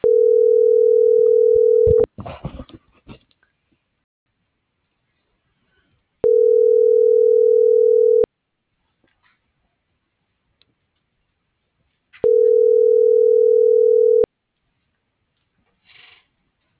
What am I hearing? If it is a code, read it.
Ringback tone (North America), 440 + 480 Hz.